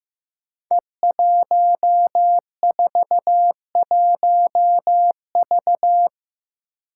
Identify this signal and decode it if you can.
Morse code, heard as a keyed tone, about 15 wpm, 700 Hz. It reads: E141V
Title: Five Challenging Letter-Number Combinations 15wpm